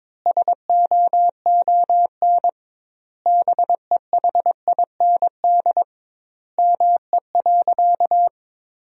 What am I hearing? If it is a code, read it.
Morse code, heard as a keyed tone, about 22 wpm, 700 Hz. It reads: SOON BEHIND ME.
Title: 2x Word-Spacing - Sentences from Top 500 Words - 22wpm